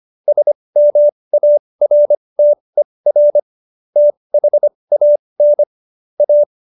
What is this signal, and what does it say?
Morse code, heard as a keyed tone, about 25 wpm, spaced as for 20 wpm, 590 Hz. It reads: SMARTER THAN A
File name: Jokes-25-char-20-eff